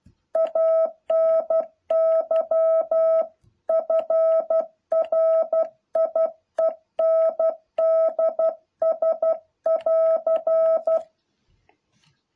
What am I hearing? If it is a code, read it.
Morse code, heard as a keyed tone, about 12 wpm, 660 Hz. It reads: ANY FRIENDS<AR>
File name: jh8cbh / 2021cw